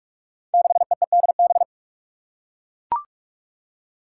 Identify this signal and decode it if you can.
Morse code, heard as a keyed tone, about 44 wpm, 700 Hz. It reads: 6EEDB
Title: Five Challenging Letter-Number Combinations 45wpm